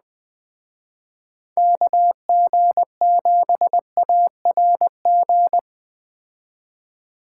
Morse code, heard as a keyed tone, about 20 wpm, 700 Hz. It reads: KG7ARG